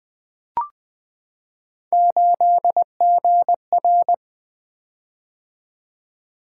Morse code, heard as a keyed tone, about 20 wpm, 700 Hz. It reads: 8GR
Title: Three Letter-Number Combinations - 20wpm